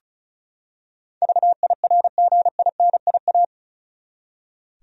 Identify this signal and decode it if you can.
Morse code, heard as a keyed tone, about 36 wpm, 700 Hz. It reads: VIRGINIA
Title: USA Common Names - 35wpm